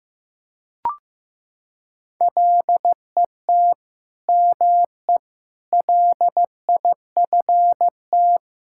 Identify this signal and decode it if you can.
Morse code, heard as a keyed tone, about 15 wpm, 700 Hz. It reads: LET ME LIFT
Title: Sentences from Top 1600 Words 15wpm